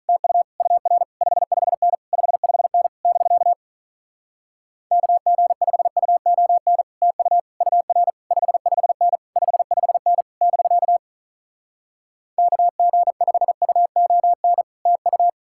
Morse code, heard as a keyed tone, about 48 wpm, 700 Hz. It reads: TU UR 55N 55N <BK> KG5UON TU UR 55N 55N <BK> KG5UON TU